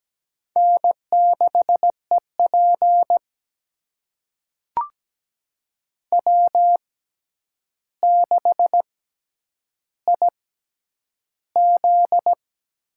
Morse code, heard as a keyed tone, about 17 wpm, spaced as for 9 wpm, 700 Hz. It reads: N6EP W6IZ
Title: USA Call Signs - 4 Characters - 6x Character Spacing - 17wpm